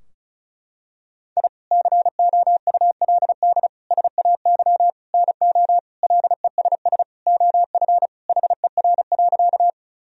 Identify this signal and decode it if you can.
Morse code, heard as a keyed tone, about 34 wpm, 700 Hz. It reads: I COULD SAY NO LESS OF HER.
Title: Sentences from Top 1000 Words - Review Part 2 - 35wpm